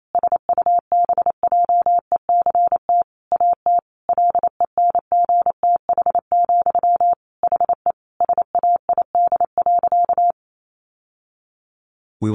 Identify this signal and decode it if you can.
Morse code, heard as a keyed tone, about 28 wpm, 700 Hz. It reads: SUBJECT AT LENGTH, HE SAID.
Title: Sentences from Top 1000 Words 28wpm